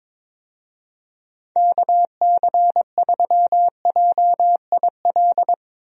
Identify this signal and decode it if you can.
Morse code, heard as a keyed tone, about 22 wpm, 700 Hz. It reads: KC3JIL
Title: USA Call Signs - 6 Characters - 22wpm